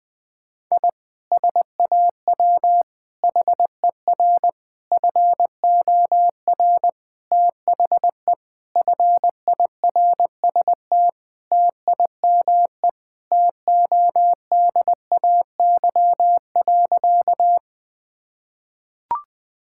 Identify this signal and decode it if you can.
Morse code, heard as a keyed tone, about 20 wpm, 700 Hz. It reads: I SAW HER FOR THE FIRST TIME TODAY.
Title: Sentences with 4-Word Phrases - 20wpm